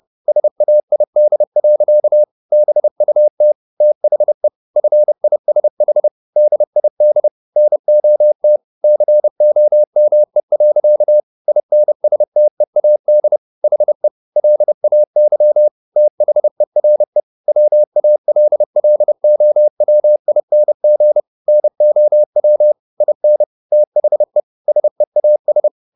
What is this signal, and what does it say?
Morse code, heard as a keyed tone, about 30 wpm, 600 Hz. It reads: SAID. BUT THE FISH DID NOT COME. INSTEAD HE LAY THERE WALLOWING NOW IN THE SEAS